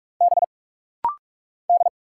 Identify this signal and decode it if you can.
Morse code, heard as a keyed tone, about 45 wpm, 700 Hz. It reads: B D
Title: Single Letter-Number Alt - Rapid-Fire 45wpm